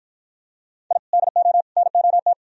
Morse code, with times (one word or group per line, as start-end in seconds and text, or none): 0.9-1.0 I
1.1-1.6 DO
1.8-2.3 NOT